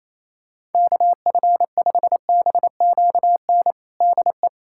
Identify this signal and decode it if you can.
Morse code, heard as a keyed tone, about 28 wpm, 700 Hz. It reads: KF5BQN DE
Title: USA Callsign DE Callsign - 6 Characters - 28wpm